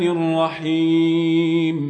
There is no code.